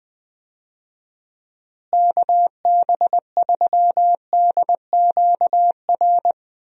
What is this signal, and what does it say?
Morse code, heard as a keyed tone, about 21 wpm, 700 Hz. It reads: KB3DQR
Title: USA Call Signs - 6 Characters - Repeated 3x - 20wpm